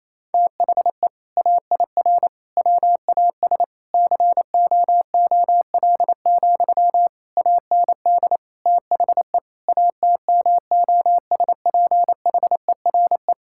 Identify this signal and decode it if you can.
Morse code, heard as a keyed tone, about 28 wpm, 700 Hz. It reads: THE AIR WAS COOL, AND THE ATMOSPHERE
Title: Sentences from Top 1600 Words 28wpm